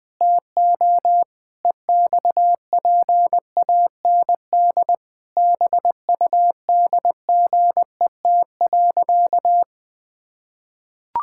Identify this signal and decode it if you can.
Morse code, heard as a keyed tone, about 20 wpm, 700 Hz. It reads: TO EXPAND BUDGET.
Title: Pangram Challenge v2 - 20wpm